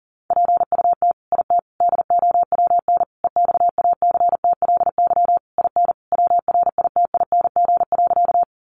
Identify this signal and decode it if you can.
Morse code, heard as a keyed tone, about 40 wpm, 700 Hz. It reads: PUT IT DOWN EXACTLY IN WRITING.
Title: Sentences from Top 900 Words 40wpm